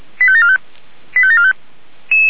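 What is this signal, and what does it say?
Five-tone selective call (ZVEI), heard as quick runs of five tones, tones of about 70 ms. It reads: 85745 85745
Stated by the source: Five-tone ZVEI selective call received by SDR.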